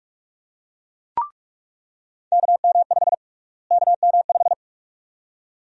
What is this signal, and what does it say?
Morse code, heard as a keyed tone, about 44 wpm, 700 Hz. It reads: KM5 KM5
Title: USA Call Sign Prefixes - Repeated 2x - 45wpm